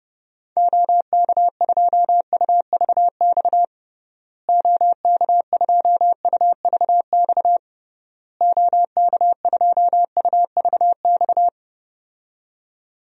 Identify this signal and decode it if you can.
Morse code, heard as a keyed tone, about 32 wpm, 700 Hz. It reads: OK2UVX OK2UVX OK2UVX
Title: DXCC-100 Call Signs - 6 Characters - Repeated 3x - 30wpm